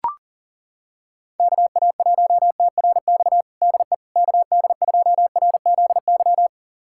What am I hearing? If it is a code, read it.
Morse code, heard as a keyed tone, about 40 wpm, 700 Hz. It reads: KA1TRX DE KD2RZY
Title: USA Callsign DE Callsign - 6 Characters - 40wpm